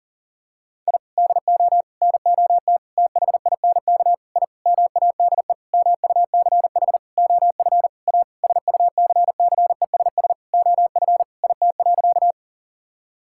Morse code, heard as a keyed tone, about 40 wpm, 700 Hz. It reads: I DO NOT THINK I MADE MUCH OF A SUCCESS OF IT.
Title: Sentences from Top 1000 Words - Review Part 6 - 40wpm